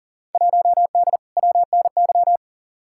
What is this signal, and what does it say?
Morse code, heard as a keyed tone, about 40 wpm, 700 Hz. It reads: 1D WNY
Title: Field Day Exchanges - 40wpm